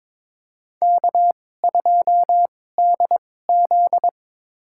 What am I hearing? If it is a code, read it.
Morse code, heard as a keyed tone, about 22 wpm, spaced as for 16 wpm, 700 Hz. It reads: K2DZ
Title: USA Call Signs - 4 Characters - 2x Character Spacing - 22wpm